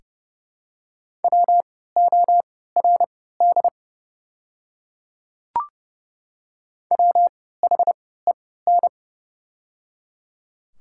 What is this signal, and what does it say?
Morse code, heard as a keyed tone, about 31 wpm, spaced as for 17 wpm, 700 Hz. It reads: WORD WHEN
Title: 3x Character Spacing - Top 100 Words - 30wpm